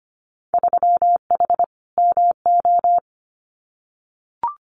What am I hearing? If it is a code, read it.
Morse code, heard as a keyed tone, about 25 wpm, 700 Hz. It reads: 3H MO
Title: Winter Field Day Exchanges Encore 25wpm